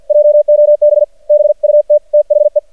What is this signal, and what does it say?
Morse code, heard as a keyed tone, about 50 wpm, 600 Hz. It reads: JOB BUT THE
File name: archived / CRSnet